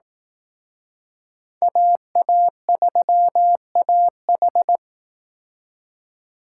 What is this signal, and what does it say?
Morse code, heard as a keyed tone, about 19 wpm, 700 Hz. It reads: AA3AH
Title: Speed Racing - USA 2x2 Call Signs 15wpm